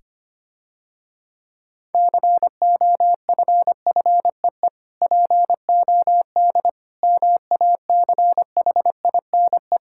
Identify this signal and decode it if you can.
Morse code, heard as a keyed tone, about 25 wpm, 700 Hz. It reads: COFFEE POD MACHINE
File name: practice / 25wpm / Difficult Words - Speed-Racing with Hints - 25wpm